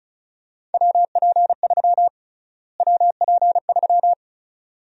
Morse code, heard as a keyed tone, about 34 wpm, 700 Hz. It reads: WP3 WP3
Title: USA Call Sign Prefixes - Repeated 2x - 35wpm